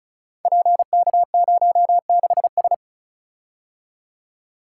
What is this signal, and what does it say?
Morse code, heard as a keyed tone, about 36 wpm, 700 Hz. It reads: PK06S
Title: Code Groups - 5 Letter:Numbers - 35wpm